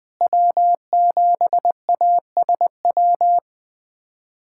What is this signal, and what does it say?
Morse code, heard as a keyed tone, about 20 wpm, 700 Hz. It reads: W7ASW
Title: USA Callsign DE Callsign - 5 Characters - 20wpm